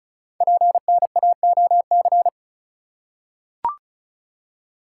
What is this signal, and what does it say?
Morse code, heard as a keyed tone, about 36 wpm, 690 Hz. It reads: PNAOC
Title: Code Groups - 5 Letters - 35wpm